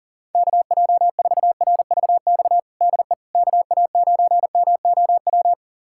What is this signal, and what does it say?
Morse code, heard as a keyed tone, about 40 wpm, 700 Hz. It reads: KJ4RVX DE KA9MOW
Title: USA Callsign DE Callsign - 6 Characters - 40wpm